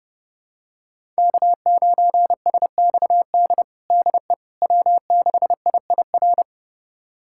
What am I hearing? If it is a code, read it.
Morse code, heard as a keyed tone, about 32 wpm, 700 Hz. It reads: K9SXD DE W6IIR
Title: USA Callsign DE Callsign - 5 Characters - 30wpm